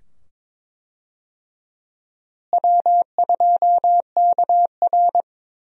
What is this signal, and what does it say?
Morse code, heard as a keyed tone, about 22 wpm, 700 Hz. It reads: W2KR
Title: USA Call Signs - 4 Characters - Repeated 2x - 22wpm